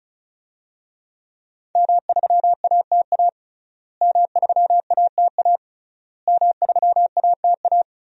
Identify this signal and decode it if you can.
Morse code, heard as a keyed tone, about 34 wpm, 700 Hz. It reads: M3ATA M3ATA M3ATA
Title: DXCC-100 Call Signs - 5 Characters - Repeated 3x - 35wpm